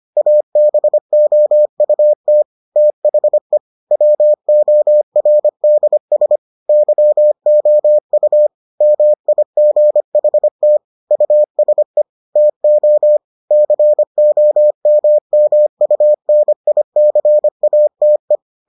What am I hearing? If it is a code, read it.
Morse code, heard as a keyed tone, about 25 wpm, 600 Hz. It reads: ABOUT THE WORDS YOU MIGHT USE TO COMMUNICATE